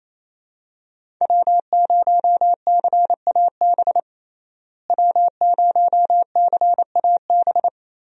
Morse code, heard as a keyed tone, about 28 wpm, 700 Hz. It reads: W0CAB W0CAB
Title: USA Call Signs - 5 Characters - Repeated 3x - 28wpm